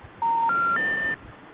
Special information tone.